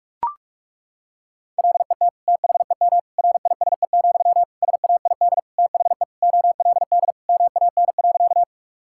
Morse code, heard as a keyed tone, about 41 wpm, 700 Hz. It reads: LET THEM RISE, SAID THE OLD MAN.
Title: Sentences from Top 1000 Words - Review Part 6 - 45wpm